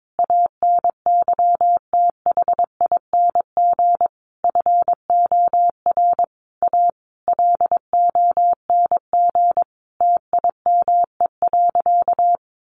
Morse code, heard as a keyed tone, about 22 wpm, 700 Hz. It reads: ANYTHING FOR A LONG TIME.
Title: Sentences from Top 1000 Words 22wpm